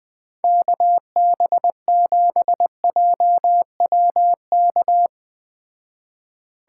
Morse code, heard as a keyed tone, about 20 wpm, 700 Hz. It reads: KB7JWK